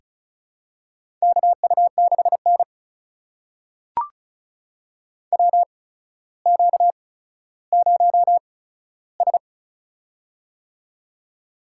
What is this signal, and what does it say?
Morse code, heard as a keyed tone, about 36 wpm, spaced as for 15 wpm, 700 Hz. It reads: KU6N WQ0S
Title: USA Call Signs - 4 Characters - 8x Character Spacing - 35wpm